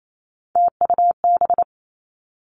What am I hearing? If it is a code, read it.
Morse code, heard as a keyed tone, about 28 wpm, 700 Hz. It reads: TUB
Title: Three Challenging Letter-Number Combinations 28wpm